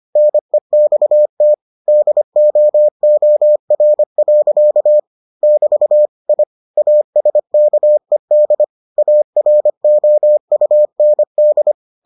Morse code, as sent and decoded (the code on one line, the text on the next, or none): -. -..- - / -.. --- --- .-. .-.-.- / -...- / .. / .- ... -.- . -.. / .- .-. --- ..- -. -..
NEXT DOOR. <BT> I ASKED AROUND